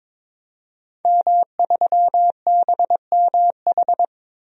Morse code, heard as a keyed tone, about 21 wpm, 700 Hz. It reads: M3BMH
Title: DXCC-100 Call Signs - 5 Characters - 22wpm